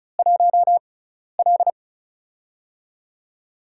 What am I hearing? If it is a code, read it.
Morse code, heard as a keyed tone, about 36 wpm, spaced as for 12 wpm, 700 Hz. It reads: 1L
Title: USA Call Signs - 4 Characters - 6x Character Spacing - 35wpm